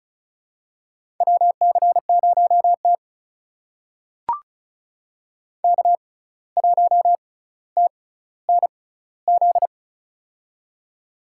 Morse code, heard as a keyed tone, about 34 wpm, spaced as for 17 wpm, 700 Hz. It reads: WC0T K1TNZ
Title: USA Call Signs - 6x Character Spacing - 35wpm